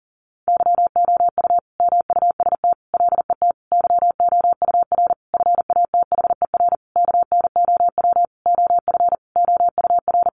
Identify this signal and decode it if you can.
Morse code, heard as a keyed tone, about 40 wpm, 700 Hz. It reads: YOU MUST LET YOUR FATHER KNOW OF OUR
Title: Sentences from Top 1500 Words 40wpm